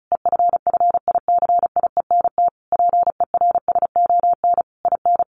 Morse code, heard as a keyed tone, about 34 wpm, 700 Hz. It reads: EFFICIENT PERSON IN
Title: Sentences from Top 1300 Words 35wpm